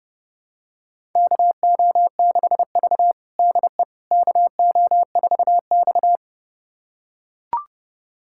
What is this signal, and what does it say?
Morse code, heard as a keyed tone, about 32 wpm, 700 Hz. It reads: KO6V DE KO4X
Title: USA Callsign DE Callsign - 4 Characters - 30wpm